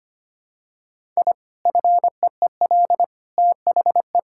Morse code, heard as a keyed tone, about 25 wpm, 700 Hz. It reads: I FEEL THE
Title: Sentences from Top 1000 Words - Review Part 6 - 25wpm